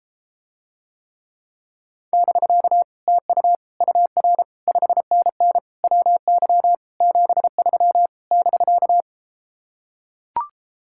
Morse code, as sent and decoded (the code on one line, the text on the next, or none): -...-.- / - ..- / ..- .-. / ..... -. -. / .-- -.-- / --... ...-- / -...-.-
<BK> TU UR 5NN WY 73 <BK>